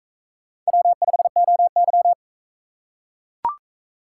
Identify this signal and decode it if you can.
Morse code, heard as a keyed tone, about 41 wpm, 700 Hz. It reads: W5OY